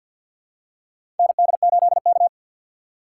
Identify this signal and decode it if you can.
Morse code, heard as a keyed tone, about 47 wpm, 700 Hz. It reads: ND8K
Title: USA Call Signs - 4 Characters - 3x Character Spacing - 50wpm